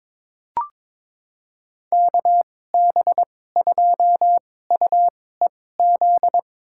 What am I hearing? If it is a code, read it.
Morse code, heard as a keyed tone, about 21 wpm, spaced as for 16 wpm, 700 Hz. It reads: KB2UEZ
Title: USA Call Signs - 6 Characters - 2x Character Spacing - 22wpm